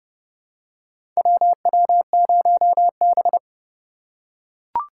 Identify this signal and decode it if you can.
Morse code, heard as a keyed tone, about 31 wpm, 700 Hz. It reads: WW0B
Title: USA Call Signs - 4 Characters - 4x Character Spacing - 30wpm